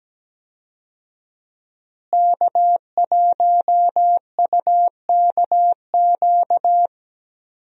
Morse code, heard as a keyed tone, about 17 wpm, 700 Hz. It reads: K1UKQ
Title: USA Call Signs - 5 Characters - Repeated 2x - 17wpm